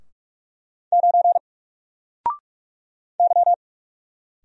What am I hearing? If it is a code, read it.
Morse code, heard as a keyed tone, about 44 wpm, 700 Hz. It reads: 9 Y
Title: Single Letter-Number Alt 45wpm